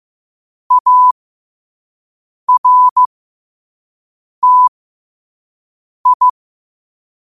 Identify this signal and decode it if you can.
Morse code, heard as a keyed tone, about 15 wpm, spaced as for 5 wpm, 1000 Hz. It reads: ARTI
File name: qotdfiles / 5wpm